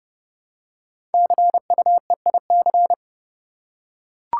Morse code, heard as a keyed tone, about 30 wpm, 700 Hz. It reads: CUEIC
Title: Code Groups - 5 Letter:Numbers - 30wpm